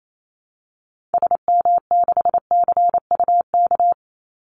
Morse code, heard as a keyed tone, about 28 wpm, 700 Hz. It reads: SM6CUK
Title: Call Signs Repeated 2x - 28wpm